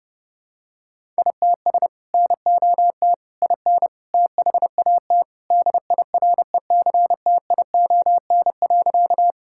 Morse code, heard as a keyed tone, about 30 wpm, 700 Hz. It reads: ITS NOT IN THAT DIRECTION.
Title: Sentences from Top 1000 Words - Review Part 5 - 30wpm